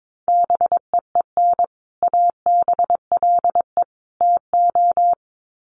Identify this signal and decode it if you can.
Morse code, heard as a keyed tone, about 22 wpm, 700 Hz. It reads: BEEN ABLE TO
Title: Sets of 3 Words 22wpm